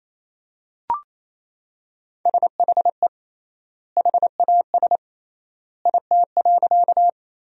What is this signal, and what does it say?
Morse code, heard as a keyed tone, about 28 wpm, 700 Hz. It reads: SHE HAS IT.
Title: 3x Word-Spacing - Sentences from Top 100 Words - 28wpm